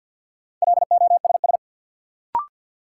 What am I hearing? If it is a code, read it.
Morse code, heard as a keyed tone, about 52 wpm, 700 Hz. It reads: LOSS
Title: Sentences from Top 1400 Words - New Word Drill - 50wpm